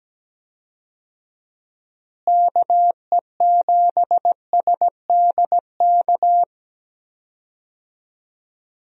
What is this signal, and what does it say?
Morse code, heard as a keyed tone, about 17 wpm, 700 Hz. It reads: KE7SDK
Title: USA Call Signs - 6 Characters - Repeated 2x - 17wpm